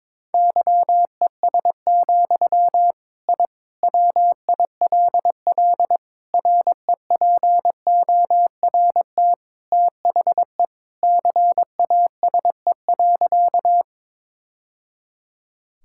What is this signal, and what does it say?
Morse code, heard as a keyed tone, about 22 wpm, 700 Hz. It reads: YES, I WILL REPORT THE CASE.
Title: Sentences from Top 1000 Words - Review Part 6 - 22wpm